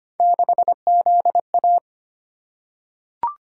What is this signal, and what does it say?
Morse code, heard as a keyed tone, about 25 wpm, 700 Hz. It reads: 6ZA
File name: Three Letter-Number Combinations - 25wpm